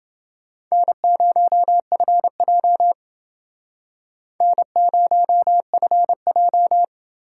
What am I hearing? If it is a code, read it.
Morse code, heard as a keyed tone, about 31 wpm, 700 Hz. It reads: N0FJ N0FJ